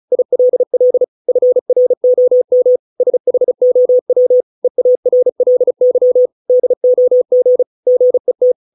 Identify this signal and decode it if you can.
Morse code, heard as a keyed tone, about 34 wpm, 500 Hz. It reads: ILL FROM SHOW EARLY DOG GET